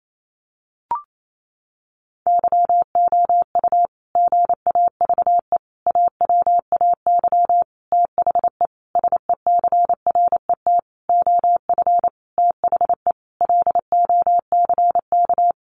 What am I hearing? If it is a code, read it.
Morse code, heard as a keyed tone, about 28 wpm, 700 Hz. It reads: YOU GAVE AWAY THE SECRET OF THE LOCK
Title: Sentences from Top 1300 Words 28wpm